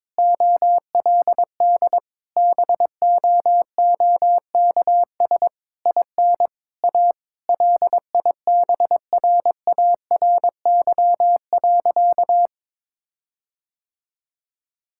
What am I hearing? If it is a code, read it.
Morse code, heard as a keyed tone, about 22 wpm, 700 Hz. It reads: OLD BOOKS IN A LIBRARY.
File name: Difficult Sentences with Hints - 22wpm